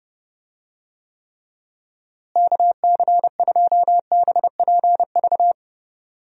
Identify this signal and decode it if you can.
Morse code, heard as a keyed tone, about 30 wpm, 700 Hz. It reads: KC2BPV
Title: USA Call Signs - 6 Characters - Repeated 2x - 30wpm